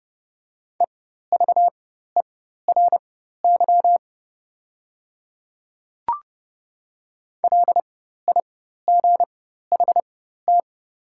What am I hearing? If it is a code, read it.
Morse code, heard as a keyed tone, about 30 wpm, spaced as for 14 wpm, 690 Hz. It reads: EVERY LIGHT
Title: 4x Character Spacing - Top 200 Words - 30wpm